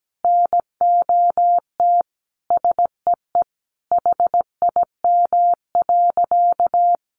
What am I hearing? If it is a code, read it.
Morse code, heard as a keyed tone, about 17 wpm, 700 Hz. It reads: NOT SEE HIM.
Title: Sentences from Top 600 Words 17wpm